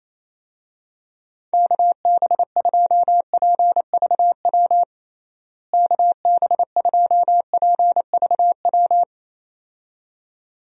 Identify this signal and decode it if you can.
Morse code, heard as a keyed tone, about 27 wpm, 700 Hz. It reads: KB2PVW KB2PVW
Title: USA Call Signs - 6 Characters - Repeated 2x - 28wpm